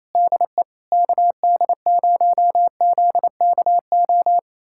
Morse code, heard as a keyed tone, about 28 wpm, 700 Hz. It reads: DE KD0ZKO
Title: USA Callsign DE Callsign - 6 Characters - 28wpm